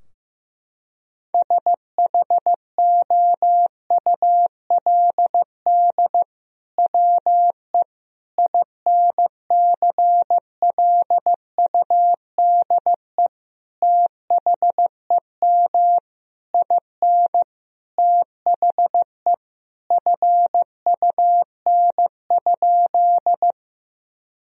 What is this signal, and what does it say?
Morse code, heard as a keyed tone, about 15 wpm, 700 Hz. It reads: SHOULD WE INCLUDE THEM IN THE FUN?
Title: Sentences from Top 1000 Words - Review Part 3 - 15wpm